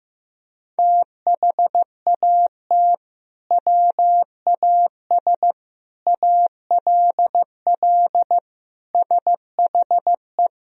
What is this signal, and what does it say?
Morse code, heard as a keyed tone, about 15 wpm, 700 Hz. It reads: THAT WAS ALL SHE
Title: Sentences from Top 100 Words and Abbreviations 15wpm